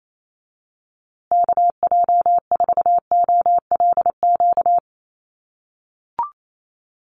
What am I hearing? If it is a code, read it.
Morse code, heard as a keyed tone, about 28 wpm, 700 Hz. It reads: KJ4OLQ
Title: USA Call Signs Repeated 2x 28wpm